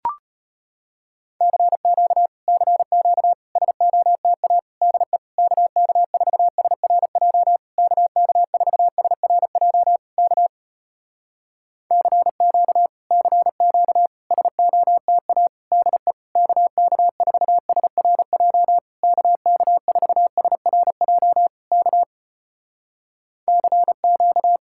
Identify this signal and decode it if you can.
Morse code, heard as a keyed tone, about 34 wpm, 700 Hz. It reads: CQ CQ SOTA DE KK4SRJ KK4SRJ K CQ CQ SOTA DE KK4SRJ KK4SRJ K CQ